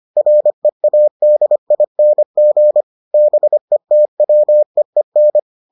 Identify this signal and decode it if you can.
Morse code, heard as a keyed tone, about 25 wpm, 600 Hz. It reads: READING BETWEEN